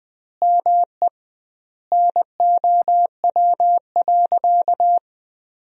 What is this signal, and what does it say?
Morse code, heard as a keyed tone, about 21 wpm, 700 Hz. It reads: ME NOW.
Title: 2x Word-Spacing - Sentences from Top 500 Words - 20wpm